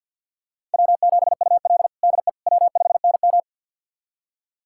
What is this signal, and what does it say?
Morse code, heard as a keyed tone, about 46 wpm, 700 Hz. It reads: W7UL DE W5NM
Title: USA Callsign DE Callsign - 4 Characters - 50wpm